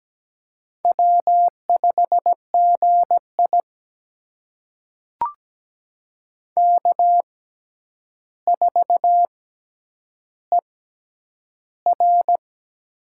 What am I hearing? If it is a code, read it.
Morse code, heard as a keyed tone, about 17 wpm, spaced as for 9 wpm, 700 Hz. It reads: W5GI K4ER
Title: USA Call Signs - 4 Characters - 6x Character Spacing - 17wpm